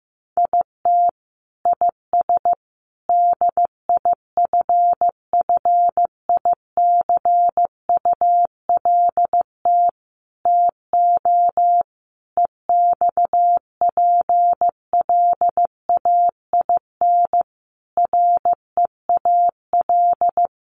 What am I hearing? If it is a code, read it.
Morse code, heard as a keyed tone, about 15 wpm, 700 Hz. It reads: IT IS DIFFICULT TO EXPLAIN REAL
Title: Sentences from Top 1100 Words 15wpm